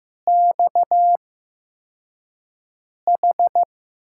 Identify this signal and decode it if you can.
Morse code, heard as a keyed tone, about 15 wpm, spaced as for 4 wpm, 700 Hz. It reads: XH
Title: USA Call Signs - 5 Characters - 8x Character Spacing - 15wpm